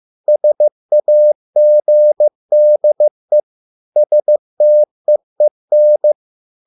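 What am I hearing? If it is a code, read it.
Morse code, heard as a keyed tone, about 15 wpm, 600 Hz. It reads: SAGDE STEEN
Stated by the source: Morse code at a slow 15 wpm from CW med Gnister.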